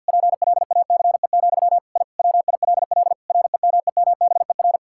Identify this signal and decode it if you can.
Morse code, heard as a keyed tone, about 51 wpm, 700 Hz. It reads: PLACE, I WILL REMEMBER
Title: Sentences from Top 1000 Words - Review Part 5 - 50wpm